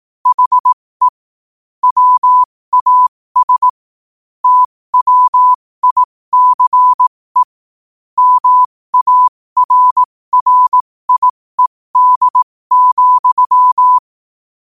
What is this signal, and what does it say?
Morse code, heard as a keyed tone, about 18 wpm, spaced as for 15 wpm, 1000 Hz. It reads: HE WAS TWICE MARRIED,
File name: qotdfiles / 15wpm